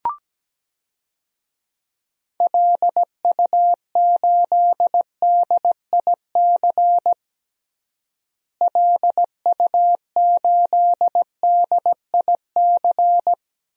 Morse code, heard as a keyed tone, about 17 wpm, 700 Hz. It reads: LU8DIC LU8DIC
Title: DXCC-100 Call Signs - 6 Characters - Repeated 2x - 17wpm